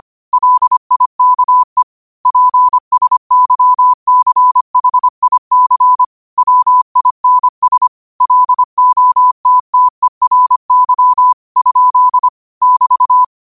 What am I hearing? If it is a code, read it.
Morse code, heard as a keyed tone, about 25 wpm, 1000 Hz. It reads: LIKE PSYCHIC WINS LOTTERY? <BT>